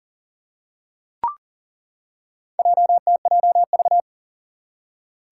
Morse code, heard as a keyed tone, about 38 wpm, 700 Hz. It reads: JTJV
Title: Four Challenging Letter-Number Combinations 40wpm